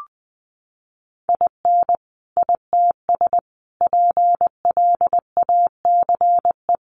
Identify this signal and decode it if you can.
Morse code, heard as a keyed tone, about 20 wpm, 700 Hz. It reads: IN ITS PLACE
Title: Sentences from Top 1400 Words 20wpm